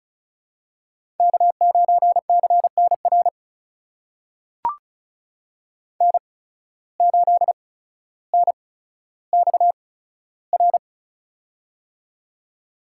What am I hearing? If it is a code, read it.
Morse code, heard as a keyed tone, about 36 wpm, spaced as for 15 wpm, 700 Hz. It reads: K9CNR N8NXR
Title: USA Call Signs - 5 Characters - 8x Character Spacing - 35wpm